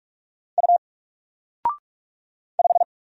Morse code, heard as a keyed tone, about 44 wpm, 700 Hz. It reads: U 5
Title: Single Character (Letter:Number:Punctuation:Prosign) 45wpm